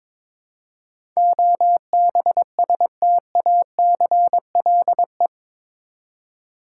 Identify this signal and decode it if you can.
Morse code, heard as a keyed tone, about 22 wpm, 700 Hz. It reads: OBSTACLE
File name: practice / 22wpm / Eight Letter Words - 22wpm